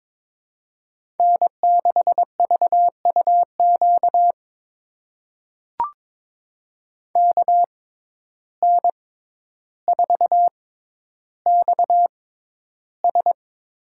Morse code, heard as a keyed tone, about 22 wpm, spaced as for 11 wpm, 700 Hz. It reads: N6VUQ KN4XS
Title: USA Call Signs - 5 Characters - 6x Character Spacing - 22wpm